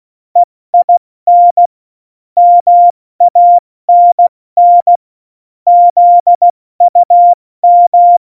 Morse code, heard as a keyed tone, about 16 wpm, 700 Hz. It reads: EIN MANN ZUM
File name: Morsekurs / Witze / Witz 06. Februar 2018, 70 F80